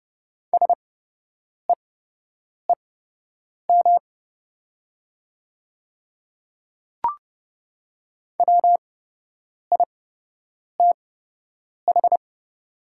Morse code, heard as a keyed tone, about 29 wpm, spaced as for 8 wpm, 690 Hz. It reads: SEEM WITH
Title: 8x Character Spacing - Top 300 Words - 30wpm